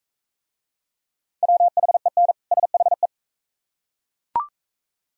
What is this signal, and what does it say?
Morse code, heard as a keyed tone, about 39 wpm, 700 Hz. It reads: WHEN SHE